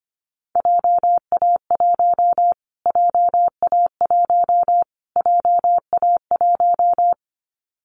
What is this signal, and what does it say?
Morse code, heard as a keyed tone, about 25 wpm, 700 Hz. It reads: JA1 JA1 JA1
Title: Callsign Prefixes for Top 25 DXCC - Repeated 3x - 25wpm